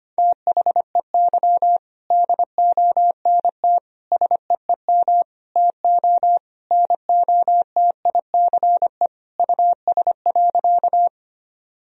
Morse code, heard as a keyed tone, about 25 wpm, 700 Hz. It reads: THEY DONT SEEM TO NOTICE US.
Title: Sentences from Top 1000 Words - Review Part 2 - 25wpm